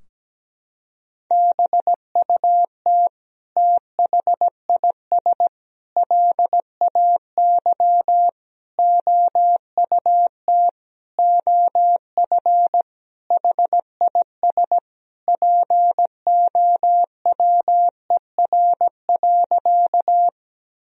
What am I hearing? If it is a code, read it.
Morse code, heard as a keyed tone, about 17 wpm, 700 Hz. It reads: BUT THIS LAY OUT OF HIS POWER.
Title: Sentences from Top 1000 Words - Review Part 2 - 17wpm